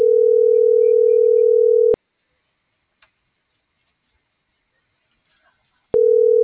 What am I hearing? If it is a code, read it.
Ringback tone (North America), 440 + 480 Hz.